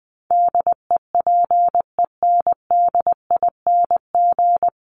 Morse code, heard as a keyed tone, about 20 wpm, 700 Hz. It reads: DEPENDING
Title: ING Suffix - Top 100 Words with Base Words 20wpm